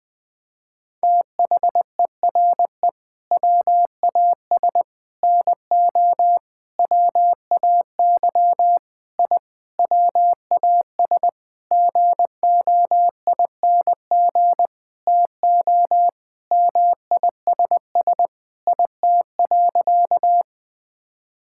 Morse code, heard as a keyed tone, about 21 wpm, 700 Hz. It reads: THERE WAS NO WAY I WAS GOING TO MISS IT.
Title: Sentences with 7-Word Phrases - 20wpm